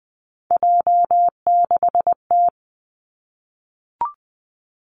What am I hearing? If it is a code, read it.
Morse code, heard as a keyed tone, about 20 wpm, 700 Hz. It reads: J6T
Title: Three Challenging Letter-Number Combinations 20wpm